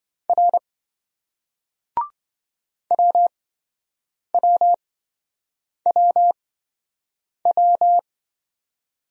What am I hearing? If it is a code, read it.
Morse code, heard as a keyed tone, about 27 wpm, spaced as for 7 wpm, 700 Hz. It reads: R WWWW